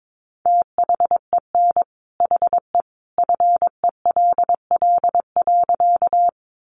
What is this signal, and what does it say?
Morse code, heard as a keyed tone, about 22 wpm, 700 Hz. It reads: THEN HE FELL.
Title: A Sound Of Thunder 22wpm